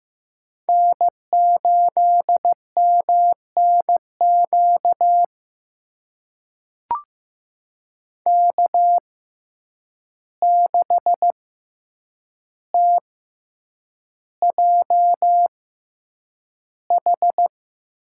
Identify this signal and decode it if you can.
Morse code, heard as a keyed tone, about 15 wpm, spaced as for 8 wpm, 700 Hz. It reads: N8MNQ K6TJH
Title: USA Call Signs - 6x Character Spacing - 15wpm